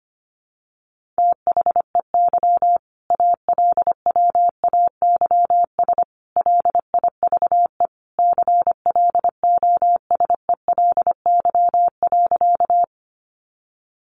Morse code, heard as a keyed tone, about 25 wpm, 700 Hz. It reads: THEY ALWAYS LIVE CLOSELY.
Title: Sentences from Top 1400 Words 25wpm